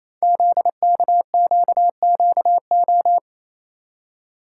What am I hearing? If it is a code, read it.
Morse code, heard as a keyed tone, about 29 wpm, 700 Hz. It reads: ZKQQO
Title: Code Groups - 5 Letters - 28wpm